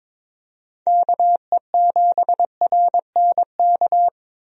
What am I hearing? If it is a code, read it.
Morse code, heard as a keyed tone, about 22 wpm, 700 Hz. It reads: KE7RNK
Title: USA Call Signs - 6 Characters - Repeated 2x - 22wpm